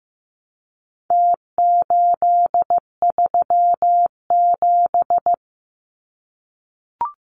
Morse code, heard as a keyed tone, about 15 wpm, 700 Hz. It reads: T837
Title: Numbers - Contest Style - 4-Digit 15wpm